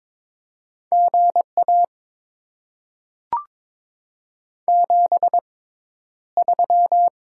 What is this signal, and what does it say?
Morse code, heard as a keyed tone, about 22 wpm, spaced as for 11 wpm, 700 Hz. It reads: GA 73
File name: Top 20 QSO Elements - 6x Character Spacing - 22wpm